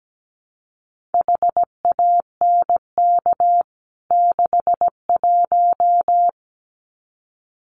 Morse code, heard as a keyed tone, about 17 wpm, 700 Hz. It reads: HANK 61
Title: CWT Exchange 17wpm